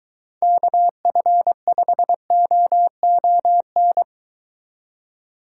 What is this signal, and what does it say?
Morse code, heard as a keyed tone, about 23 wpm, 700 Hz. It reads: KF5OON